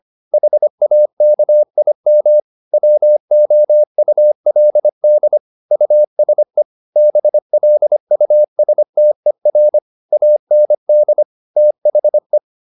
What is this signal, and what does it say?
Morse code, heard as a keyed tone, about 25 wpm, 600 Hz. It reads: HAKIM WOULD USE BLUSTER AND THE
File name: Never 0025wpm